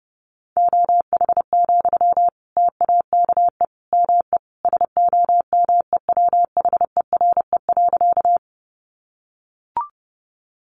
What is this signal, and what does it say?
Morse code, heard as a keyed tone, about 30 wpm, 700 Hz. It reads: OH, TAKE ME SOMEWHERE.
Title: Sentences from Top 1400 Words 30wpm